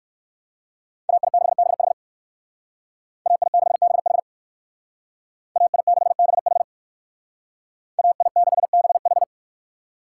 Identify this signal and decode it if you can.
Morse code, heard as a keyed tone, about 58 wpm, 700 Hz. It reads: AI6BH AI6BH AI6BH AI6BH